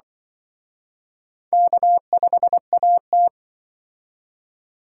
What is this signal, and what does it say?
Morse code, heard as a keyed tone, about 24 wpm, 700 Hz. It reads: K5AT
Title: Speed Racing - USA 1x2 Call Signs 20wpm